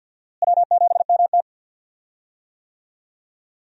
Morse code, heard as a keyed tone, about 52 wpm, 700 Hz. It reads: W7MT
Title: USA Call Signs - 4 Characters - Repeated 2x - 50wpm